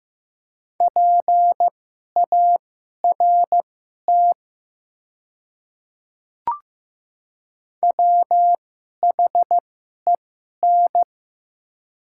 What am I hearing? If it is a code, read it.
Morse code, heard as a keyed tone, about 15 wpm, spaced as for 11 wpm, 700 Hz. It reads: PART WHEN
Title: 2x Character Spacing - Top 500 Words - 15wpm